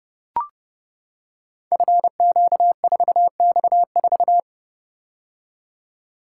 Morse code, heard as a keyed tone, about 30 wpm, 700 Hz. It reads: FQ4X4
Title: Code Groups - 5 Letter:Numbers - 30wpm